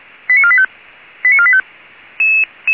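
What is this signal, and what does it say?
Five-tone selective call (ZVEI), heard as quick runs of five tones, tones of about 70 ms. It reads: 89485 89485
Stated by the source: ZVEI five-tone fire brigade alert received with an SDR, German fire brigade archive, 464 MHz.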